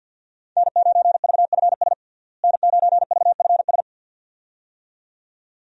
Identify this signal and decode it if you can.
Morse code, heard as a keyed tone, about 49 wpm, 700 Hz. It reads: N9VFS N9VFS
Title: USA Call Signs - 5 Characters - Repeated 2x - 50wpm